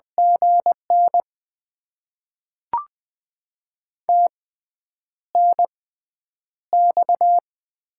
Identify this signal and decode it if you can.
Morse code, heard as a keyed tone, about 20 wpm, spaced as for 9 wpm, 700 Hz. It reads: GN TNX
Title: Top 20 QSO Elements - 6x Character Spacing - 20wpm